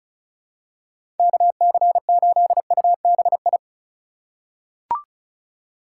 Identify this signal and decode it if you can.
Morse code, heard as a keyed tone, about 36 wpm, 700 Hz. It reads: KC8UBI
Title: USA Call Signs - 6 Characters - 2x Character Spacing - 35wpm